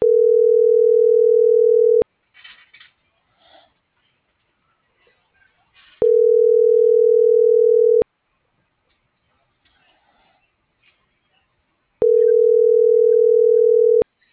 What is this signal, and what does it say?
Ringback tone (North America), 440 + 480 Hz.